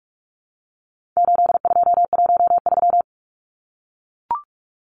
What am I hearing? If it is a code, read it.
Morse code, heard as a keyed tone, about 44 wpm, 700 Hz. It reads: 8213
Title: Numbers - 4-Digit 45wpm